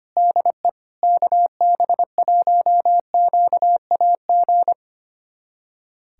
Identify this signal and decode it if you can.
Morse code, heard as a keyed tone, about 25 wpm, 700 Hz. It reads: DE KB1QAG
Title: USA Callsign DE Callsign 25wpm